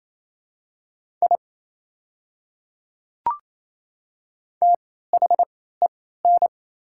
Morse code, heard as a keyed tone, about 28 wpm, spaced as for 16 wpm, 690 Hz. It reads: I THEN
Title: 3x Character Spacing - Top 100 Words - 28wpm